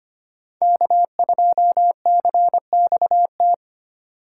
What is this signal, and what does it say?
Morse code, heard as a keyed tone, about 33 wpm, 700 Hz. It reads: K2CXT